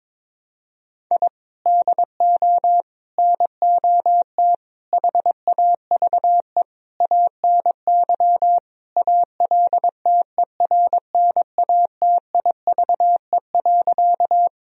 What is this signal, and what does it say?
Morse code, heard as a keyed tone, about 22 wpm, 700 Hz. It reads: I DO NOT HAVE ANY ALTERNATIVE.
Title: Sentences from Top 1000 Words - Review Part 4 - 22wpm